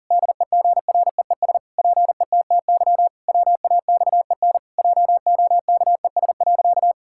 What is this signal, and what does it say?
Morse code, heard as a keyed tone, about 40 wpm, 700 Hz. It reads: DEGREES PETTY WAXEN JOKES.
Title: Pangram Challenge v2 - 40wpm